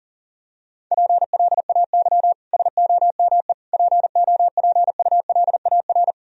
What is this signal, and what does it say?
Morse code, heard as a keyed tone, about 40 wpm, 700 Hz. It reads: PLAY SOME POPULAR
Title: Sentences from Top 1000 Words - Review Part 3 - 40wpm